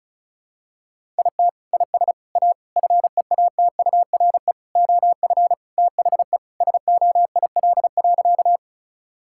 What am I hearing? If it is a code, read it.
Morse code, heard as a keyed tone, about 36 wpm, 700 Hz. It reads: IT IS A FEATURE OF THE SOIL.
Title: Sentences from Top 1000 Words - Review Part 5 - 35wpm